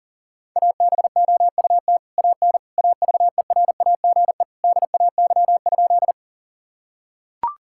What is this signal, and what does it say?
Morse code, heard as a keyed tone, about 40 wpm, 700 Hz. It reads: ABOUT AN AVERAGE DAY?
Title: Sentences from Top 1000 Words - Review Part 6 - 40wpm